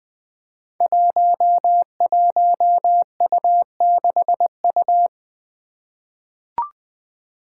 Morse code, heard as a keyed tone, about 20 wpm, 700 Hz. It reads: 11U6U
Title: Five Challenging Letter-Number Combinations 20wpm